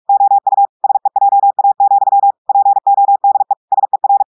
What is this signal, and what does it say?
Morse code, heard as a keyed tone, about 46 wpm, 800 Hz. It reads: OU SEJA, PODE SER
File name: audios / portugais / portugais texte en clair 2 vitesse 45 mots minutes